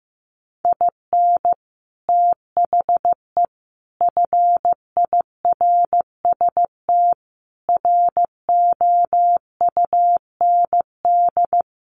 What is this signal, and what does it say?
Morse code, heard as a keyed tone, about 15 wpm, 700 Hz. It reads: IN THE FIRST ROUND
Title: Sets of 5 Words Encore 15wpm